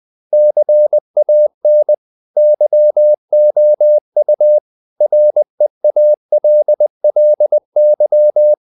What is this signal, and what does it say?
Morse code, heard as a keyed tone, about 20 wpm, 600 Hz. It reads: CAN YOU REALLY
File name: the old man and the sea / the old man and the sea part 1 20wpm